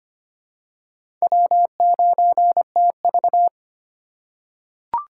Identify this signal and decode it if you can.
Morse code, heard as a keyed tone, about 25 wpm, 700 Hz. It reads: W9TV
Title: USA Call Signs - 4 Characters - 3x Character Spacing - 25wpm